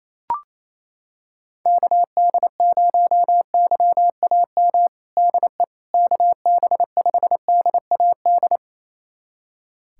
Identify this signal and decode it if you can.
Morse code, heard as a keyed tone, about 28 wpm, 700 Hz. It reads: KD0YAM DE KB5DAD
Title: USA Callsign DE Callsign - 6 Characters - 28wpm